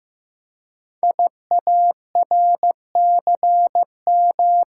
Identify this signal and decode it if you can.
Morse code, heard as a keyed tone, about 15 wpm, 700 Hz. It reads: IARCM
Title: Code Groups - 5 Letters - 15wpm